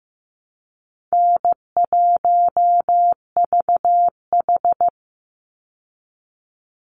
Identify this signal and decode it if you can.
Morse code, heard as a keyed tone, about 15 wpm, 700 Hz. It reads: N1VH
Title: Call Signs Repeated 3x - 15wpm